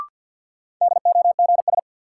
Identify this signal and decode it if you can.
Morse code, heard as a keyed tone, about 53 wpm, 700 Hz. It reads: DOGS
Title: Sentences from Top 1600 Words - New Word Drill Alt - 50wpm